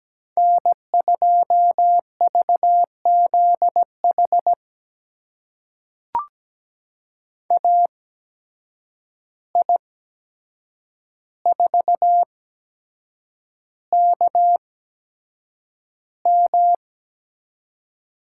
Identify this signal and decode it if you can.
Morse code, heard as a keyed tone, about 17 wpm, spaced as for 7 wpm, 700 Hz. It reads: N2VZH AI4KM
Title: USA Call Signs - 5 Characters - 8x Character Spacing - 17wpm